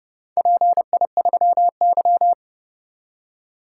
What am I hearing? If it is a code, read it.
Morse code, heard as a keyed tone, about 27 wpm, 700 Hz. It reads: PI3Y